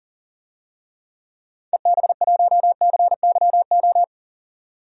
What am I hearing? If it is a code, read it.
Morse code, heard as a keyed tone, about 40 wpm, 700 Hz. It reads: EB1CYO
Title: DXCC-100 Call Signs - 6 Characters - 40wpm